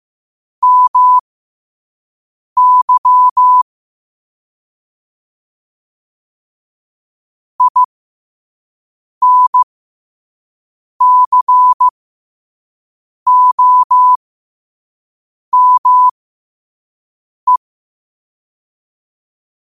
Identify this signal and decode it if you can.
Morse code, heard as a keyed tone, about 15 wpm, spaced as for 5 wpm, 1000 Hz. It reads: MY INCOME
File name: qotdfiles / 5wpm